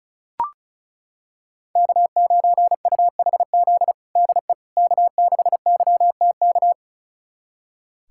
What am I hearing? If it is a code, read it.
Morse code, heard as a keyed tone, about 34 wpm, 700 Hz. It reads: K9UHZ DE K6YTK
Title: USA Callsign DE Callsign - 5 Characters - 35wpm